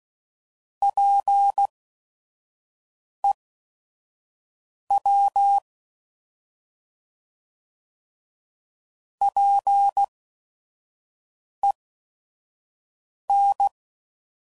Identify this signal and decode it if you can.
Morse code, heard as a keyed tone, about 16 wpm, spaced as for 5 wpm, 800 Hz. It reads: PEW PEN
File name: Prefixes / By First Letter / Ps